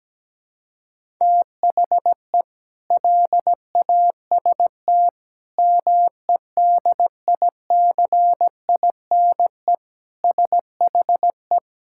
Morse code, heard as a keyed tone, about 17 wpm, 700 Hz. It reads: THE LAST MEDICINE SHE
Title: Sentences from Top 1000 Words - Review Part 6 - 17wpm